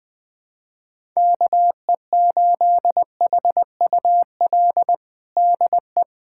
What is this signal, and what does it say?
Morse code, heard as a keyed tone, about 20 wpm, 700 Hz. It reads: KE8HUL DE
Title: USA Callsign DE Callsign 20wpm